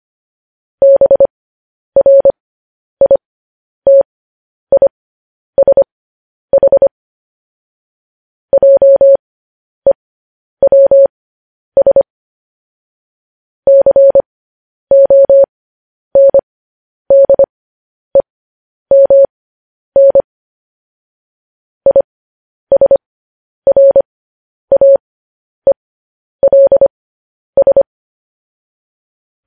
Morse code, heard as a keyed tone, about 25 wpm, spaced as for 10 wpm, 550 Hz. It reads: BRITISH JEWS CONDEMN ISRAELS
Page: News Headlines in Morse Code at 10 WPM